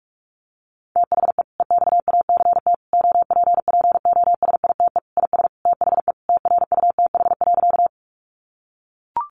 Morse code, heard as a keyed tone, about 44 wpm, 700 Hz. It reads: THE EXACT OPPOSITE IS THE TRUTH.
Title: Sentences from Top 1300 Words 45wpm